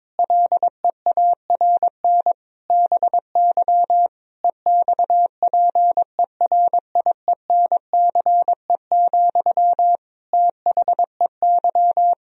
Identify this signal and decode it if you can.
Morse code, heard as a keyed tone, about 22 wpm, 700 Hz. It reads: LEARN BY EXPERIENCE, THEY
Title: Sentences from Top 1000 Words - Review Part 3 - 22wpm